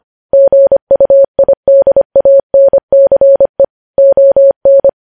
Morse code, heard as a keyed tone, about 25 wpm, 550 Hz. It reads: GUIDANCE ON
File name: podcast / News headlines-1x-25x25